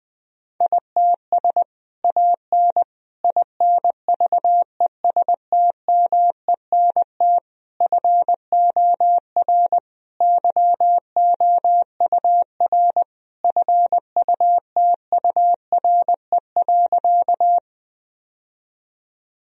Morse code, heard as a keyed tone, about 20 wpm, 700 Hz. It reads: ITS AN INVESTMENT FOR YOUR FUTURE.
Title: Sentences from Top 1000 Words - Review Part 4 - 20wpm